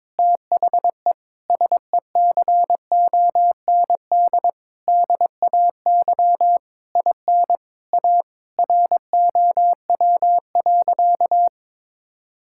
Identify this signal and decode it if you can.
Morse code, heard as a keyed tone, about 22 wpm, 700 Hz. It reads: THE SECOND DAY IN A ROW.
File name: Sentences with 7-Word Phrases - 22wpm